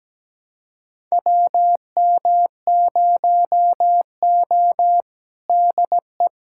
Morse code, heard as a keyed tone, about 17 wpm, 700 Hz. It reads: WM0O DE
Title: USA Callsign DE Callsign - 4 Characters - 17wpm